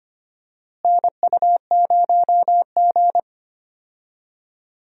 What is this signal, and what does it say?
Morse code, heard as a keyed tone, about 25 wpm, 700 Hz. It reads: NU0G
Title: USA Call Signs - 4 Characters - Repeated 2x - 25wpm